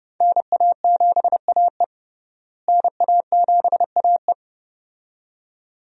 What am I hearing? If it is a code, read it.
Morse code, heard as a keyed tone, about 31 wpm, 700 Hz. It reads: NA7AE NA7AE
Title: USA Call Signs - 5 Characters - Repeated 2x - 30wpm